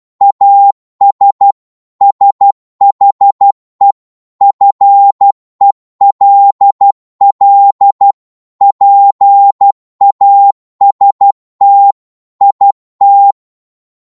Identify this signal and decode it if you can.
Morse code, heard as a keyed tone, about 12 wpm, 800 Hz. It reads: AS SHE FELL PAST IT
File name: cwmp3 / psg-1-12wpm